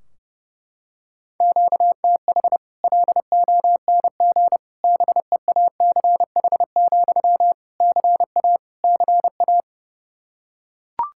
Morse code, heard as a keyed tone, about 30 wpm, 700 Hz. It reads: QTH LONG BEACH, CA CA
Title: Practice QSO - Exchange 1 - Beginning - 30wpm